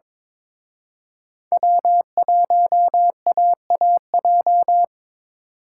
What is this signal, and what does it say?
Morse code, heard as a keyed tone, about 27 wpm, 700 Hz. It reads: W1AAJ
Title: Speed Racing - USA 1x3 Call Signs 22wpm